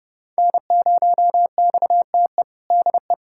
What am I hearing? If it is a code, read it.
Morse code, heard as a keyed tone, about 29 wpm, 700 Hz. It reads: N0XTE DE
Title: USA Callsign DE Callsign 30wpm